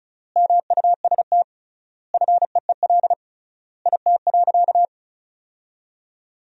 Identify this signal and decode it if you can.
Morse code, heard as a keyed tone, about 34 wpm, 700 Hz. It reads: MUST FEEL IT.
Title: 3x Word-Spacing - Sentences from Top 400 Words - 35wpm